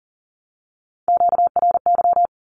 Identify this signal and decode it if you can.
Morse code, heard as a keyed tone, about 40 wpm, 700 Hz. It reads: QRY
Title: Q-Codes 40wpm